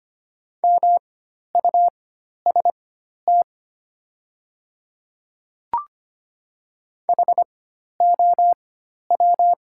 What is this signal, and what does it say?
Morse code, heard as a keyed tone, about 25 wpm, spaced as for 12 wpm, 700 Hz. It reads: MUST HOW
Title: 4x Character Spacing - Top 200 Words - 25wpm